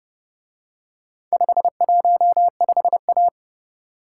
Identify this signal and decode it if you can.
Morse code, heard as a keyed tone, about 29 wpm, 700 Hz. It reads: 515A